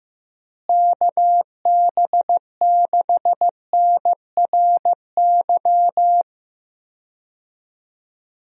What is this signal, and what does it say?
Morse code, heard as a keyed tone, about 15 wpm, 700 Hz. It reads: KB6NRY
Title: USA Call Signs - 6 Characters - Repeated 3x - 15wpm